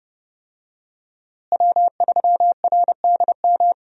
Morse code, heard as a keyed tone, about 31 wpm, 700 Hz. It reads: W3RDM
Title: USA Call Signs - 5 Characters - Repeated 2x - 30wpm